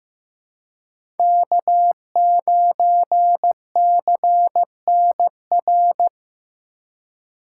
Morse code, heard as a keyed tone, about 15 wpm, 700 Hz. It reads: K9CNR
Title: USA Call Signs - 5 Characters - 2x Character Spacing - 15wpm